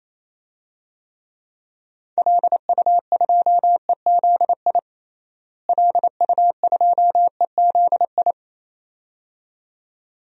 Morse code, heard as a keyed tone, about 28 wpm, 700 Hz. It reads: LU2EZI LU2EZI
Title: DXCC-100 Call Signs - 6 Characters - Repeated 2x - 28wpm